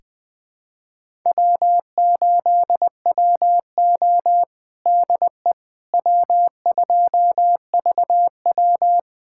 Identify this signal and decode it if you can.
Morse code, heard as a keyed tone, about 20 wpm, 700 Hz. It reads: W8WO DE W2VW
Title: USA Callsign DE Callsign - 4 Characters - 20wpm